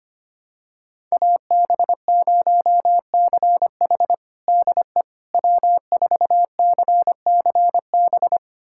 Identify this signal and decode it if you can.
Morse code, heard as a keyed tone, about 25 wpm, 700 Hz. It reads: AB0CH DE W4CCB
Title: USA Callsign DE Callsign - 5 Characters - 25wpm